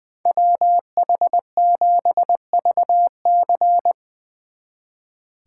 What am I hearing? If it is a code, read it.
Morse code, heard as a keyed tone, about 23 wpm, 700 Hz. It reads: WH7VC
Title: Speed Racing - USA Call Signs 20wpm